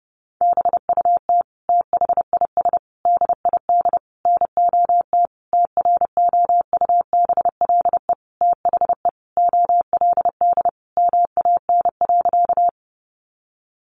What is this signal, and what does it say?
Morse code, heard as a keyed tone, about 30 wpm, 700 Hz. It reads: BUT THIS DID NOT TROUBLE THE OLD MAN.
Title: Sentences from Top 1200 Words 30wpm